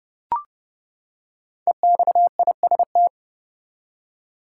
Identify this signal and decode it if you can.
Morse code, heard as a keyed tone, about 31 wpm, 700 Hz. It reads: EXIST
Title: Five Letter Words - 30wpm